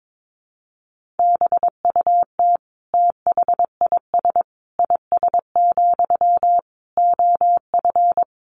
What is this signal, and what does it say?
Morse code, heard as a keyed tone, about 22 wpm, 700 Hz. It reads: BUT THIS IS, OF
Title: Sentences from Top 1500 Words 22wpm